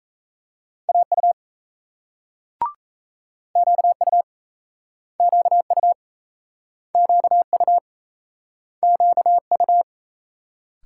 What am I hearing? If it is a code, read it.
Morse code, heard as a keyed tone, about 38 wpm, 700 Hz. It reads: AU QU QU QU QU